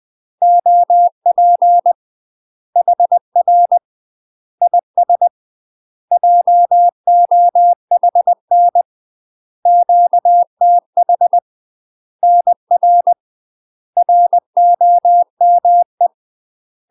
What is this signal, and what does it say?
Morse code, heard as a keyed tone, about 20 wpm, 700 Hz. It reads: OP HR IS JOHN QTH NR ROME
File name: UK MASTER QSOs TRAINING / 20WPM / UK MASTER QSOs 0052